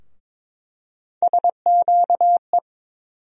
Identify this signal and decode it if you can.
Morse code, heard as a keyed tone, about 21 wpm, 700 Hz. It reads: SQE
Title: USA Call Sign Postfixes - 3 Letters 22wpm